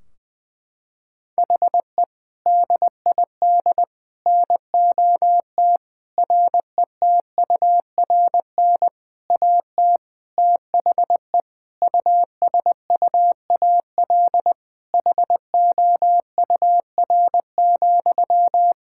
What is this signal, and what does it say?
Morse code, heard as a keyed tone, about 21 wpm, 700 Hz. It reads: HE DID NOT RETURN AT THE USUAL HOUR,
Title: Sentences from Top 1600 Words 20wpm